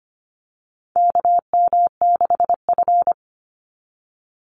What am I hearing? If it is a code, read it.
Morse code, heard as a keyed tone, about 25 wpm, 700 Hz. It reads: KM6F
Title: USA Call Signs 25wpm